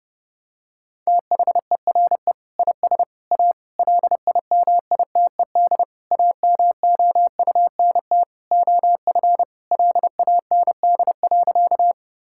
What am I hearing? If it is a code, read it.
Morse code, heard as a keyed tone, about 32 wpm, 700 Hz. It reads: THERE IS A LIMITED AMOUNT OF LAND.
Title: Sentences from Top 1000 Words - Review Part 6 - 30wpm